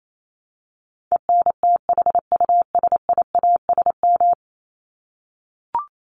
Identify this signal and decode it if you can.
Morse code, heard as a keyed tone, about 28 wpm, 700 Hz. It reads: ENTHUSIASM
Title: Top 1751-2000 Words 28wpm